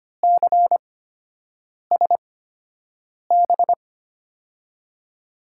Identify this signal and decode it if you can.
Morse code, heard as a keyed tone, about 25 wpm, spaced as for 7 wpm, 700 Hz. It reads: CSB
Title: USA Call Signs - 8x Character Spacing - 25wpm